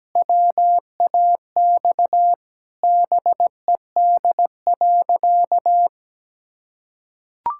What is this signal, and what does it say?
Morse code, heard as a keyed tone, about 17 wpm, 700 Hz. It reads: WAX BED.
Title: Pangram Challenge v2 - 17wpm